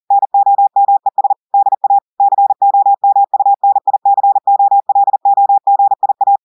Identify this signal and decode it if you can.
Morse code, heard as a keyed tone, about 40 wpm, 800 Hz. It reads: NOMES DA COMUNICOLOGIA
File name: audios / portugais / portugais texte en clair 1 vitesse 40 mots minutes